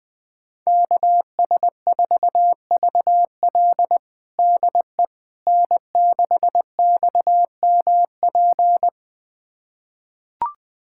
Morse code, heard as a keyed tone, about 21 wpm, 700 Hz. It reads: KS4VL DE N6XMP
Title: USA Callsign DE Callsign - 5 Characters - 20wpm